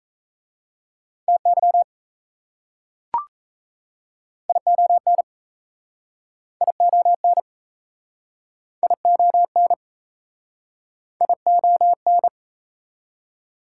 Morse code, heard as a keyed tone, about 38 wpm, 690 Hz. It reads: TY ION ION ION ION